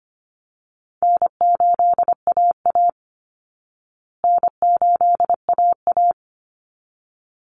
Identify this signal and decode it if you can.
Morse code, heard as a keyed tone, about 25 wpm, 700 Hz. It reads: N8AA N8AA
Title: Easy Call Signs Repeated 2x - 25wpm